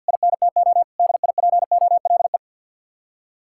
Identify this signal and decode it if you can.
Morse code, heard as a keyed tone, about 66 wpm, spaced as for 58 wpm, 700 Hz. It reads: INTO DIPOLE
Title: Speed-Racing Common QSO Phrases - 50wpm